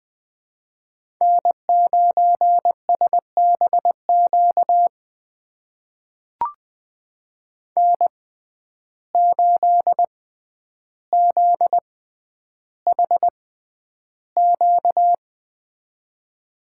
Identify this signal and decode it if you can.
Morse code, heard as a keyed tone, about 20 wpm, spaced as for 10 wpm, 700 Hz. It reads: N9SBQ N8ZHQ
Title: USA Call Signs - 5 Characters - 6x Character Spacing - 20wpm